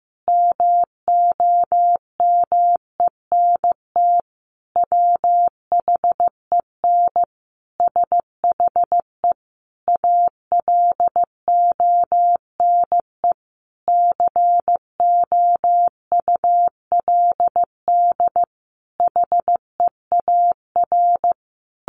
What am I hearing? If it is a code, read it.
Morse code, heard as a keyed tone, about 15 wpm, 700 Hz. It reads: MOMENT WHEN SHE ALONE COULD HEAR
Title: Sentences from Top 1000 Words 15wpm